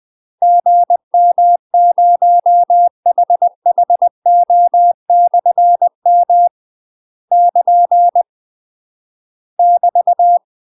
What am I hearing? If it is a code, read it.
Morse code, heard as a keyed tone, about 20 wpm, 700 Hz. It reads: GM0HHO/M <KN> <BT>
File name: UK MASTER QSOs TRAINING/20WPM